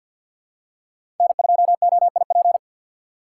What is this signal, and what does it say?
Morse code, heard as a keyed tone, about 47 wpm, 700 Hz. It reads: N2OIP
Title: USA Call Signs - 5 Characters - 3x Character Spacing - 50wpm